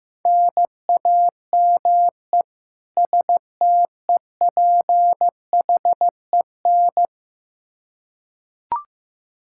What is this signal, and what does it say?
Morse code, heard as a keyed tone, about 15 wpm, 700 Hz. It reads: NAME STEPHEN
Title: Practice QSO - Exchange 1 - Beginning - 15wpm